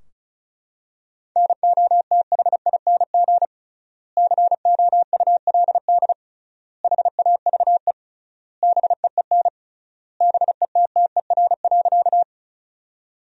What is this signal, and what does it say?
Morse code, heard as a keyed tone, about 36 wpm, 700 Hz. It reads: NOTHING COULD HAVE BEEN BETTER.
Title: 3x Word-Spacing - Sentences from Top 500 Words - 35wpm